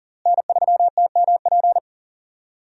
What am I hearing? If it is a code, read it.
Morse code, heard as a keyed tone, about 40 wpm, 700 Hz. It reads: N3TMP
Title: USA Call Signs - 5 Characters - Repeated 2x - 40wpm